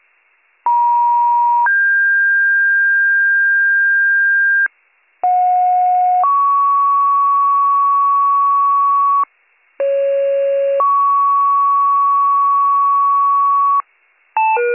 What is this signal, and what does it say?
Two-tone page, heard as a one-second tone, then a three-second tone, about 950 then 1600 Hz, about 730 then 1120 Hz, about 550 then 1120 Hz. It is heard from 0.7 to 13.8 s.